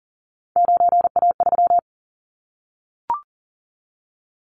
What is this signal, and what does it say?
Morse code, heard as a keyed tone, about 42 wpm, 700 Hz. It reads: 9A3
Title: Callsign Prefixes for Top 50 DXCC - 40wpm